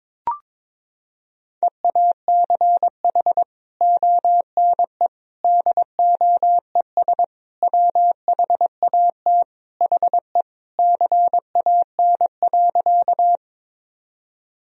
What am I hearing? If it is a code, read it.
Morse code, heard as a keyed tone, about 21 wpm, 700 Hz. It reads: EACH ONE DOES WHAT HE CAN.
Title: Sentences from Top 1000 Words - Review Part 1 - 22wpm